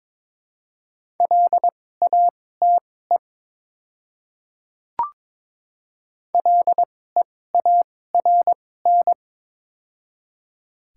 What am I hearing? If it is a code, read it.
Morse code, heard as a keyed tone, about 23 wpm, spaced as for 16 wpm, 700 Hz. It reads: LATE LEARN
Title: 2x Character Spacing - Top 500 Words - 22wpm